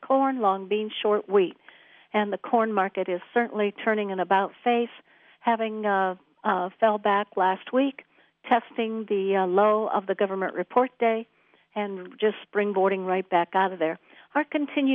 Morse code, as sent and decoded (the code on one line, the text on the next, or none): none